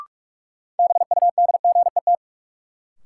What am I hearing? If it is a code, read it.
Morse code, heard as a keyed tone, about 42 wpm, 700 Hz. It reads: BUDGET
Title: Sentences from Top 1200 Words - New Word Drill - 45wpm